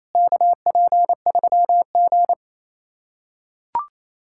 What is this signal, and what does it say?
Morse code, heard as a keyed tone, about 28 wpm, 700 Hz. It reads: KP3G
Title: USA Callsign DE Callsign - 4 Characters - 28wpm